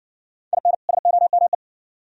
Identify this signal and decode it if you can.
Morse code, heard as a keyed tone, about 60 wpm, 700 Hz. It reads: IN SOME